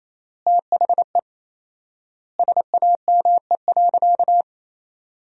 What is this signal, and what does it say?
Morse code, heard as a keyed tone, about 27 wpm, 700 Hz. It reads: THE SAME.
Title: 4x Word-Spacing - Sentences from Top 300 Words - 28wpm